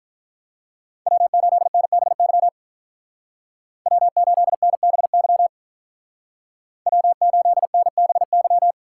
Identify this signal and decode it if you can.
Morse code, heard as a keyed tone, about 41 wpm, 700 Hz. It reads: W8NBY W8NBY W8NBY
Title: Speed Racing - USA 1x3 Call Signs 35wpm